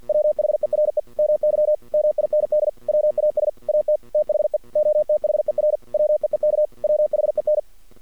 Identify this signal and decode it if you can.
Morse code, heard as a keyed tone, about 53 wpm, 610 Hz. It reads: WHERE MY MIND WAS AT THE OTHER WEEK WHEN
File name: archived / CRSnet